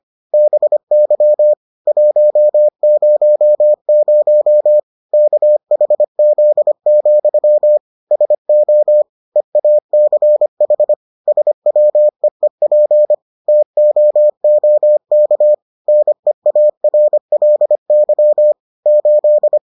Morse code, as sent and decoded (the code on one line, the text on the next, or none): -... -.-- / .---- ----- ----- / -.- .... --.. --..-- / ... --- / . .- -.-. .... / ... .-- . .--. / - --- --- -.- / -. .- .-. .-.. -.-- / ---..
BY 100 KHZ, SO EACH SWEEP TOOK NEARLY 8